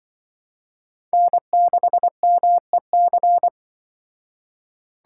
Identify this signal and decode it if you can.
Morse code, heard as a keyed tone, about 24 wpm, 700 Hz. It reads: N6MEC
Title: Speed Racing - USA 1x3 Call Signs 20wpm